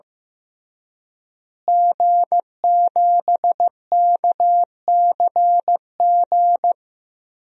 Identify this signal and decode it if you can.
Morse code, heard as a keyed tone, about 15 wpm, 700 Hz. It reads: G7KCG
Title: DXCC-100 Call Signs - 5 Characters - Repeated 3x - 15wpm